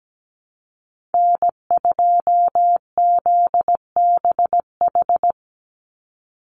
Morse code, heard as a keyed tone, about 17 wpm, 700 Hz. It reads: N2ZBH
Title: USA Call Signs 17wpm